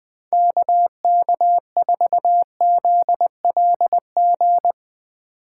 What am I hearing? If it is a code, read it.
Morse code, heard as a keyed tone, about 20 wpm, 700 Hz. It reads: KK4ZLG
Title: Speed Racing - USA 2x3 Call Signs 17wpm